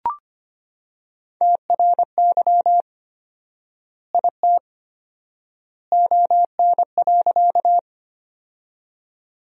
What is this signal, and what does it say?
Morse code, heard as a keyed tone, about 25 wpm, 700 Hz. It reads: TRY IT ON.
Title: 4x Word-Spacing - Sentences from Top 200 Words - 25wpm